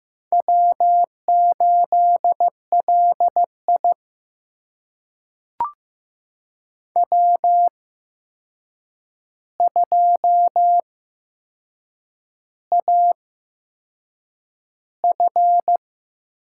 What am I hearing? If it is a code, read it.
Morse code, heard as a keyed tone, about 15 wpm, spaced as for 6 wpm, 700 Hz. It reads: W8LI W2AF
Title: USA Call Signs - 4 Characters - 8x Character Spacing - 15wpm